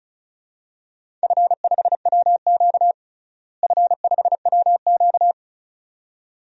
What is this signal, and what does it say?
Morse code, heard as a keyed tone, about 36 wpm, 700 Hz. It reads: F5WQ F5WQ
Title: DXCC-100 Call Signs - 4 Characters - Repeated 2x - 35wpm